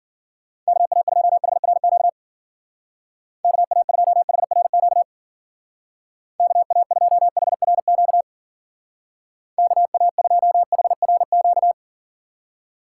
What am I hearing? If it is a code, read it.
Morse code, heard as a keyed tone, about 51 wpm, 700 Hz. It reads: KA2HRQ KA2HRQ KA2HRQ KA2HRQ